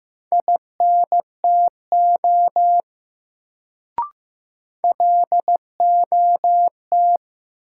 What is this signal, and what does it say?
Morse code, heard as a keyed tone, about 15 wpm, 700 Hz. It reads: INTO LOT
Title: Sets of 2 Words - New Word Drill - 15wpm